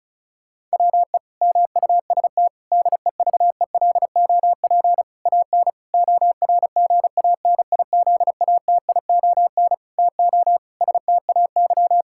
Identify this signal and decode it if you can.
Morse code, heard as a keyed tone, about 34 wpm, 700 Hz. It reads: WE MUST DEVELOP AN ORGANIZATION TO STAY
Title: Sentences from Top 1000 Words - Review Part 5 - 35wpm